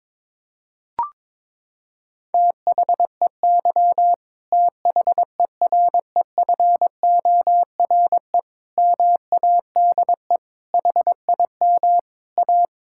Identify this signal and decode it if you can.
Morse code, heard as a keyed tone, about 22 wpm, 700 Hz. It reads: THEY THEREFORE MADE HIM A
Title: Sentences from Top 1000 Words - Review Part 5 - 22wpm